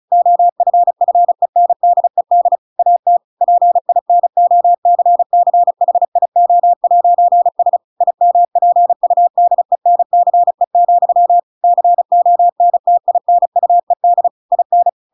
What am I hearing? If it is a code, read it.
Morse code, heard as a keyed tone, about 35 wpm, 700 Hz. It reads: OFFENDED AT PINOCCHIO'S IMPUDENCE, CONTINUED IN